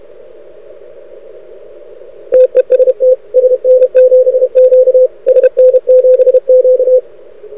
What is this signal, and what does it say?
Morse code, heard as a keyed tone, about 32 wpm, 500 Hz. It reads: TEST SN7Q SN7Q
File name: rec contest / arrl cw 2018 / test 32